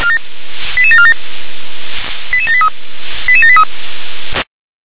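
Five-tone selective call (ZVEI), heard as quick runs of five tones, tones of about 70 ms. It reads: 88747 88783 88783
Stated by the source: ZVEI five-tone call 88702, received on SDR.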